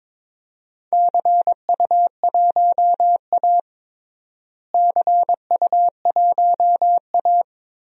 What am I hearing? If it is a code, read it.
Morse code, heard as a keyed tone, about 22 wpm, 700 Hz. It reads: CU1A CU1A
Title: DXCC-100 Call Signs - 4 Characters - Repeated 3x - 22wpm